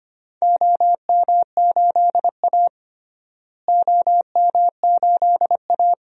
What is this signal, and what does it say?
Morse code, heard as a keyed tone, about 25 wpm, 700 Hz. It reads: OM8A OM8A
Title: DXCC-100 Call Signs - 4 Characters - Repeated 2x - 25wpm